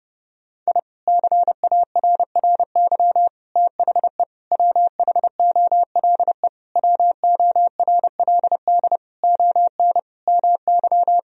Morse code, heard as a keyed tone, about 30 wpm, 700 Hz. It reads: I CARRY THE WHOLE WORLD ON MY
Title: Sentences from Top 1000 Words - Review Part 1 - 30wpm